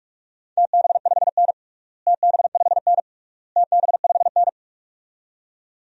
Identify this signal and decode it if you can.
Morse code, heard as a keyed tone, about 46 wpm, 700 Hz. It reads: TB5N TB5N TB5N
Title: DXCC-100 Call Signs - 4 Characters - Repeated 3x - 45wpm